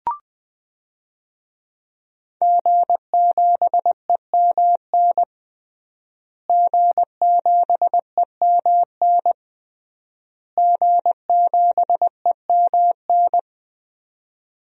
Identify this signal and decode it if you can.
Morse code, heard as a keyed tone, about 20 wpm, 700 Hz. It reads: G7EMN G7EMN G7EMN
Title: DXCC-100 Call Signs - 5 Characters - Repeated 3x - 20wpm